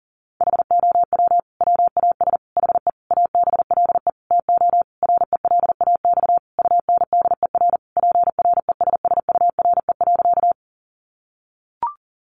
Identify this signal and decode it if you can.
Morse code, heard as a keyed tone, about 40 wpm, 700 Hz. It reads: HOW WAS HE ABLE TO RELAX UNDER PRESSURE.
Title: Sentences from Top 1400 Words 40wpm